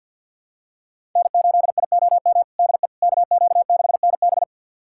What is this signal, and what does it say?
Morse code, heard as a keyed tone, about 45 wpm, 690 Hz. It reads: N8IOM DE KQ6NB
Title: USA Callsign DE Callsign - 5 Characters - 50wpm